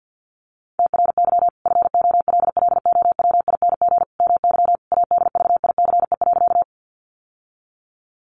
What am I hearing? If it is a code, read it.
Morse code, heard as a keyed tone, about 50 wpm, 700 Hz. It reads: TRY FOLLOWING MY ADVICE.
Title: Sentences from Top 1100 Words 50wpm